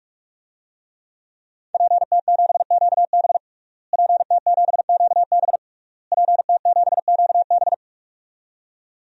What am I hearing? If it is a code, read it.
Morse code, heard as a keyed tone, about 46 wpm, 700 Hz. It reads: PT7QB PT7QB PT7QB
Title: DXCC-100 Call Signs - 5 Characters - Repeated 3x - 45wpm